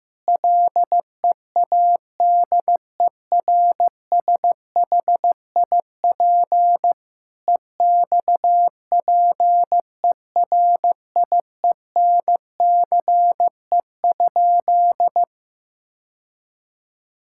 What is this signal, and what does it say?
Morse code, heard as a keyed tone, about 15 wpm, 700 Hz. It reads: LEADERSHIP EXPERIENCE?
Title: Sentences from Top 1600 Words 15wpm